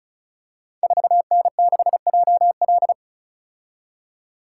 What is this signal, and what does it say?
Morse code, heard as a keyed tone, about 36 wpm, 690 Hz. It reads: 4N6JL
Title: Code Groups - 5 Letter:Numbers - 35wpm